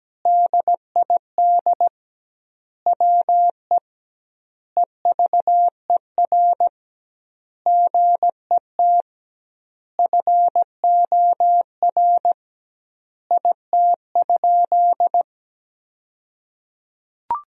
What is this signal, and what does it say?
Morse code, heard as a keyed tone, about 17 wpm, 700 Hz. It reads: DID WE EVER GET FOR IT?
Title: 2x Word-Spacing - Sentences from Top 400 Words - 17wpm